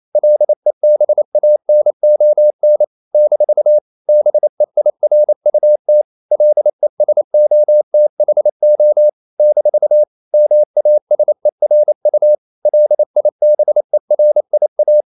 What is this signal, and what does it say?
Morse code, heard as a keyed tone, about 28 wpm, 600 Hz. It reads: LEBANON - BEIRUT LESOTHO - MASERU LIBERIA
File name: CW / Practice / NationalCapitals28